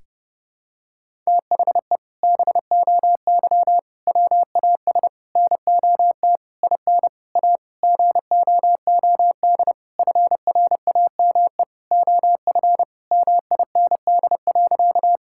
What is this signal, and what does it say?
Morse code, heard as a keyed tone, about 30 wpm, 700 Hz. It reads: THE BOY WAS NOT IN A GOOD FRAME OF MIND.
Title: Sentences from Top 1000 Words - Review Part 6 - 30wpm